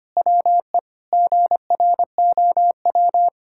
Morse code, heard as a keyed tone, about 25 wpm, 700 Hz. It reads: WE GROW
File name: practice / 25wpm / Difficult Sentences with Hints - 25wpm